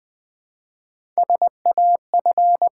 Morse code, heard as a keyed tone, about 20 wpm, 700 Hz. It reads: SAF
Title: Build A Word - 1 Letter At A Time - 4-Letter Words - 20wpm